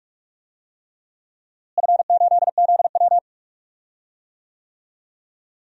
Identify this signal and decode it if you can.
Morse code, heard as a keyed tone, about 46 wpm, 700 Hz. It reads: F8ZW
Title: DXCC-100 Call Signs - 4 Characters - Repeated 3x - 45wpm